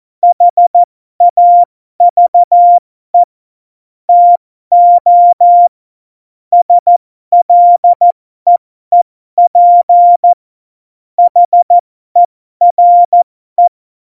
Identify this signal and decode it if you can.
Morse code, heard as a keyed tone, about 14 wpm, spaced as for 12 wpm, 700 Hz. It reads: HAVE TO SLEEP HERE